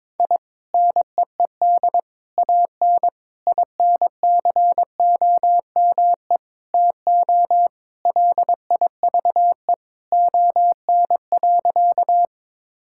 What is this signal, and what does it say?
Morse code, heard as a keyed tone, about 22 wpm, 700 Hz. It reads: I NEED AN INCOME TO LIVE ON.
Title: Sentences from Top 1000 Words - Review Part 5 - 22wpm